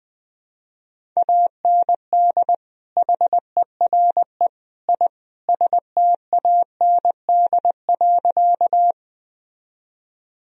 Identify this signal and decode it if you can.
Morse code, heard as a keyed tone, about 21 wpm, 700 Hz. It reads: AND HERE I STAND.
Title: Sentences from Top 1000 Words - Review Part 1 - 20wpm